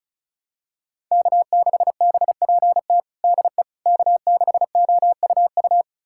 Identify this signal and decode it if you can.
Morse code, heard as a keyed tone, about 36 wpm, 700 Hz. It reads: K6BPT DE K6OUU
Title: USA Callsign DE Callsign - 5 Characters - 35wpm